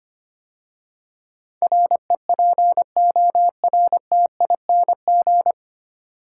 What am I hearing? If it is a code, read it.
Morse code, heard as a keyed tone, about 25 wpm, 700 Hz. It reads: REPORTING
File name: practice / 25wpm / 9 Letter Words - First 3 Letters Given - 25wpm